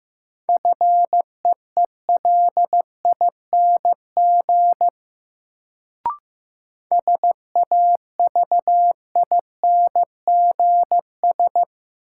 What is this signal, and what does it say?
Morse code, heard as a keyed tone, about 15 wpm, 700 Hz. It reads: FEELING SAVINGS
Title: Sentences from Top 1200 Words - New Word Drill Alt - 15wpm